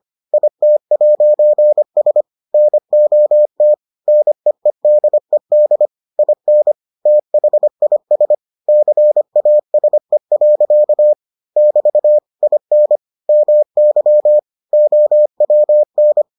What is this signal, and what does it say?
Morse code, heard as a keyed tone, about 25 wpm, 600 Hz. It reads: IT'S NOT NEEDED IN THIS CASE. <BT> IN MY OWN